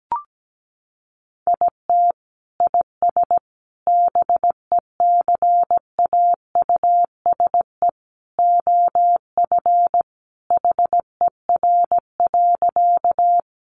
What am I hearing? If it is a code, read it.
Morse code, heard as a keyed tone, about 17 wpm, 700 Hz. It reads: IT IS BECAUSE OF HER.
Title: Sentences from Top 600 Words 17wpm